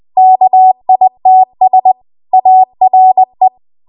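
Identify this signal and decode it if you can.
Morse code, heard as a keyed tone, about 20 wpm, 750 Hz. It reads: KITS ARE
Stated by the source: Morse code practice at 20 WPM.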